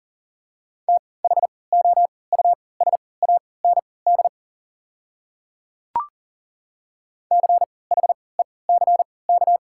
Morse code, heard as a keyed tone, about 40 wpm, spaced as for 23 wpm, 700 Hz. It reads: THOUSAND CHECK
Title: 3x Character Spacing - Top 500 Words - 40wpm